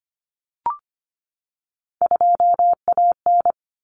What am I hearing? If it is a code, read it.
Morse code, heard as a keyed tone, about 25 wpm, 700 Hz. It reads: 2AN
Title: Common Cut Numbers - 3-Digit 25wpm